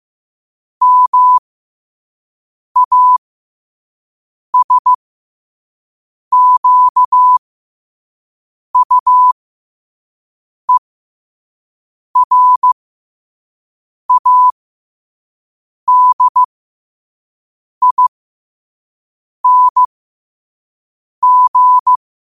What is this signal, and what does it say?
Morse code, heard as a keyed tone, about 15 wpm, spaced as for 5 wpm, 1000 Hz. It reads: MASQUERADING
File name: qotdfiles / 5wpm